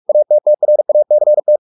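Morse code, heard as a keyed tone, about 44 wpm, 600 Hz. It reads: ATTRACT
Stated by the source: Morse practice at 45 wpm, the novel Never by Ken Follett in English, from CW med Gnister.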